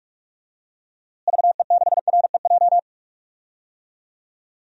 Morse code, heard as a keyed tone, about 46 wpm, 700 Hz. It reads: VE6REJ
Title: DXCC-100 Call Signs - 6 Characters - Repeated 3x - 45wpm